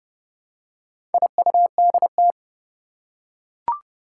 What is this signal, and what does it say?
Morse code, heard as a keyed tone, about 29 wpm, 690 Hz. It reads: IUDT
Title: Four Challenging Letter-Number Combinations 30wpm